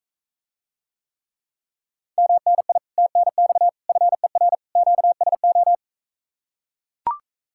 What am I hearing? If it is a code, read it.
Morse code, heard as a keyed tone, about 44 wpm, 700 Hz. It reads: MNI TNX FER QSO